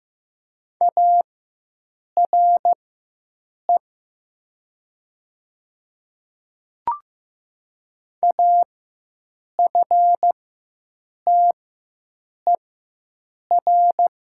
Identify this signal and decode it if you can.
Morse code, heard as a keyed tone, about 15 wpm, spaced as for 7 wpm, 700 Hz. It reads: ARE AFTER
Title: 4x Character Spacing - Top 200 Words - 15wpm